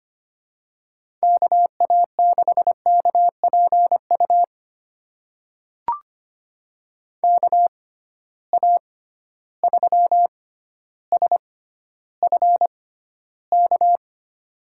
Morse code, heard as a keyed tone, about 25 wpm, spaced as for 13 wpm, 700 Hz. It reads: KA6KPU KA3SFK
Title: USA Call Signs - 6 Characters - 6x Character Spacing - 25wpm